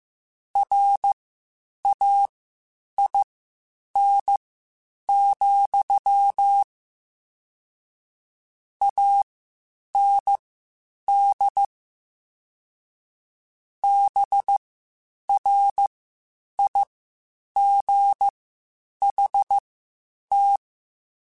Morse code, heard as a keyed tone, about 15 wpm, spaced as for 8 wpm, 800 Hz. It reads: RAIN, AND BRIGHT